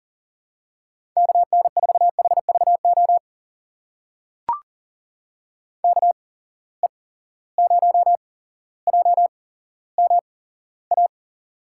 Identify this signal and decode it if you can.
Morse code, heard as a keyed tone, about 39 wpm, spaced as for 17 wpm, 700 Hz. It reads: KN4HVO KE0JMA
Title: USA Call Signs - 6 Characters - 8x Character Spacing - 40wpm